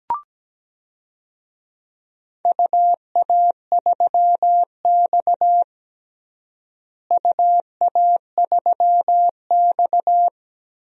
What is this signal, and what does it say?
Morse code, heard as a keyed tone, about 17 wpm, 700 Hz. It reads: UA3X UA3X
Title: DXCC-100 Call Signs - 4 Characters - Repeated 2x - 17wpm